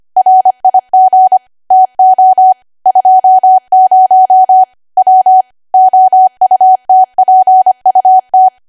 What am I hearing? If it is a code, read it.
Morse code, heard as a keyed tone, about 25 wpm, 750 Hz. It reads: RIG TO 20 W OUTPUT
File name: Archive / 25 WPM / 25WPM